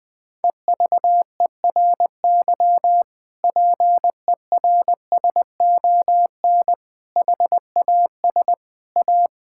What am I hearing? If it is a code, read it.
Morse code, heard as a keyed tone, about 20 wpm, 700 Hz. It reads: EVERY PERSON HAS A
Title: Sentences from Top 1000 Words - Review Part 5 - 20wpm